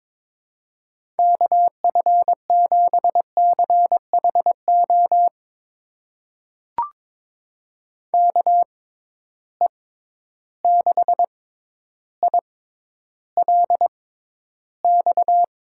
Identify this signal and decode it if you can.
Morse code, heard as a keyed tone, about 21 wpm, spaced as for 11 wpm, 700 Hz. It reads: KF7CHO KE6ILX
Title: USA Call Signs - 6 Characters - 6x Character Spacing - 22wpm